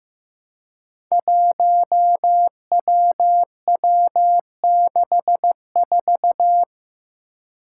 Morse code, heard as a keyed tone, about 15 wpm, 700 Hz. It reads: 1WW64
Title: Five Challenging Letter-Number Combinations 15wpm